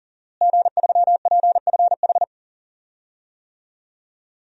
Morse code, heard as a keyed tone, about 40 wpm, 700 Hz. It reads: G3PFH
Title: DXCC-100 Call Signs - 5 Characters - Repeated 3x - 40wpm